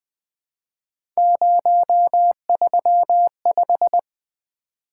Morse code, heard as a keyed tone, about 20 wpm, 700 Hz. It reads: 035
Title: Three Letter-Number Combinations - 20wpm